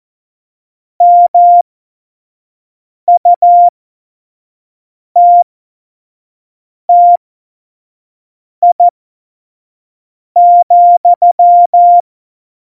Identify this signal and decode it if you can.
Morse code, heard as a keyed tone, about 14 wpm, spaced as for 5 wpm, 700 Hz. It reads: MUTTI,
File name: Morsekurs / Witze / Witz 28. November 2017, 25 F70